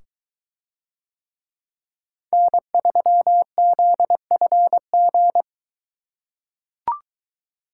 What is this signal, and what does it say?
Morse code, heard as a keyed tone, about 22 wpm, 700 Hz. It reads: N3ZFG